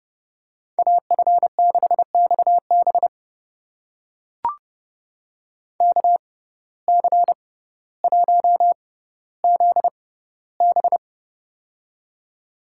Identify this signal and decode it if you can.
Morse code, heard as a keyed tone, about 31 wpm, spaced as for 16 wpm, 700 Hz. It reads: AF6XB KC1ZB
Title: USA Call Signs - 5 Characters - 6x Character Spacing - 30wpm